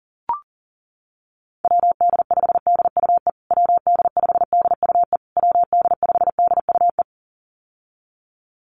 Morse code, heard as a keyed tone, about 40 wpm, 700 Hz. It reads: WD5DUE WD5DUE WD5DUE
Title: USA Call Signs Repeated 3x 40wpm